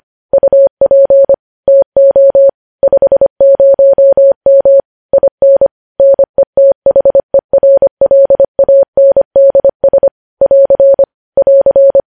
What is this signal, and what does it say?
Morse code, heard as a keyed tone, about 25 wpm, 550 Hz. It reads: UP TO 50M IN NETHERLANDS <AR> <AR>
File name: podcast / News headlines-1x-25x25